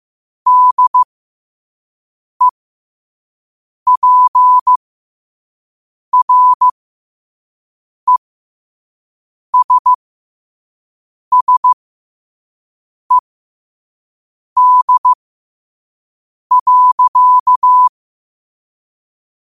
Morse code, heard as a keyed tone, about 15 wpm, spaced as for 5 wpm, 1000 Hz. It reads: DEPRESSED.
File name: qotdfiles / 5wpm